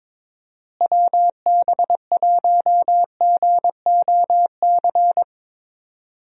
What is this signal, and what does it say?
Morse code, heard as a keyed tone, about 22 wpm, 700 Hz. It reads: WB1GOC
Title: USA Call Signs - 6 Characters - Repeated 3x - 22wpm